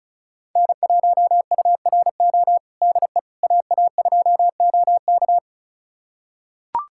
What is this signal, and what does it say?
Morse code, heard as a keyed tone, about 36 wpm, 700 Hz. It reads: N1URO DE AA2OK
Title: USA Callsign DE Callsign - 5 Characters - 35wpm